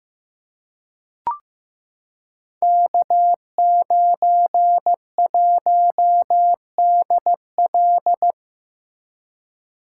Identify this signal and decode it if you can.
Morse code, heard as a keyed tone, about 15 wpm, 700 Hz. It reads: K91DL
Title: Code Groups - 5 Letter:Numbers - 15wpm